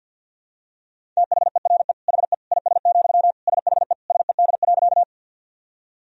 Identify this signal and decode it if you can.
Morse code, heard as a keyed tone, about 52 wpm, 700 Hz. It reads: THERE HE IS, SHE SED.
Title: Sentences from Top 100 Words and Abbreviations 50wpm